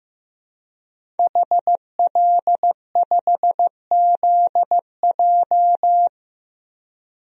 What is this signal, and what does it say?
Morse code, heard as a keyed tone, about 15 wpm, 700 Hz. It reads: HL5ZJ
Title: DXCC-100 Call Signs - 5 Characters - 15wpm